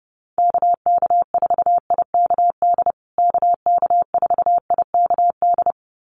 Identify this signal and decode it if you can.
Morse code, heard as a keyed tone, about 30 wpm, 700 Hz. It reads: KK4IKD KK4IKD
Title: USA Call Signs Repeated 2x 30wpm